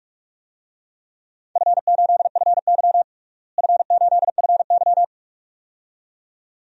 Morse code, heard as a keyed tone, about 46 wpm, 700 Hz. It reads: F8FY F8FY
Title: DXCC-100 Call Signs - 4 Characters - Repeated 2x - 45wpm